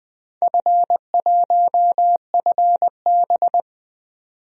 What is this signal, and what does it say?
Morse code, heard as a keyed tone, about 20 wpm, 700 Hz. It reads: F1FB
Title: DXCC-100 Call Signs - 4 Characters - 20wpm